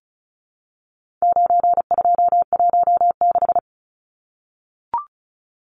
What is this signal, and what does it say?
Morse code, heard as a keyed tone, about 34 wpm, 700 Hz. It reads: 9216
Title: Numbers - 4-Digit 35wpm